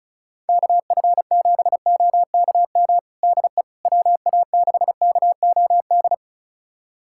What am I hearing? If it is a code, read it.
Morse code, heard as a keyed tone, about 34 wpm, 700 Hz. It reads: KF7OKM DE WA6KOD
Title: USA Callsign DE Callsign - 6 Characters - 35wpm